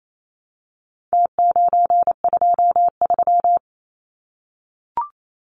Morse code, heard as a keyed tone, about 28 wpm, 700 Hz. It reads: T923
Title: Numbers - Contest Style - 4-Digit 28wpm